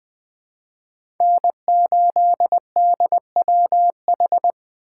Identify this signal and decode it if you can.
Morse code, heard as a keyed tone, about 20 wpm, 700 Hz. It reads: N8DWH
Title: USA Call Signs - 4x Character Spacing - 20wpm